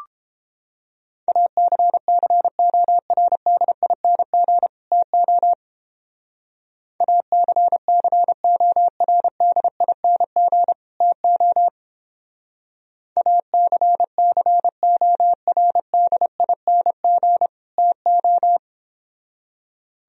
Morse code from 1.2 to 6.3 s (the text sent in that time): ACCORDING TO